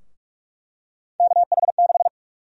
Morse code, heard as a keyed tone, about 48 wpm, 700 Hz. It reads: KH6
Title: USA Call Sign Prefixes - Repeated 2x - 45wpm